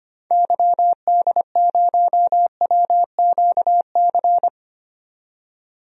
Morse code, heard as a keyed tone, about 25 wpm, 700 Hz. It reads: YD0WQC
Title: DXCC-100 Call Signs - 6 Characters - 25wpm